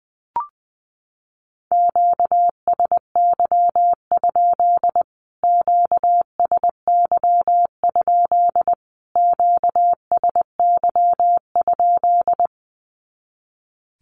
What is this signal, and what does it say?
Morse code, heard as a keyed tone, about 20 wpm, 700 Hz. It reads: QSY? QSY? QSY?
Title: Q-Codes Repeated 3x 20wpm